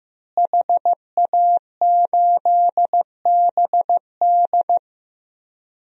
Morse code, heard as a keyed tone, about 15 wpm, 700 Hz. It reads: HA8BD
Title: DXCC-100 Call Signs - 5 Characters - Repeated 2x - 15wpm